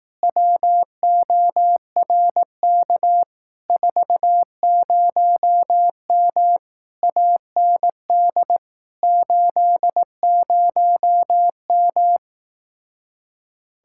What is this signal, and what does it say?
Morse code, heard as a keyed tone, about 18 wpm, 700 Hz. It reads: WORK 40M AND 80M
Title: Speed-Racing Common QSO Phrases Encore - 15wpm